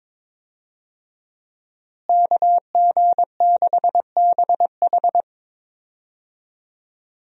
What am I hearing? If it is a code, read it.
Morse code, heard as a keyed tone, about 22 wpm, 700 Hz. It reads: KG6BH
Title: USA Call Signs - 5 Characters - Repeated 2x - 22wpm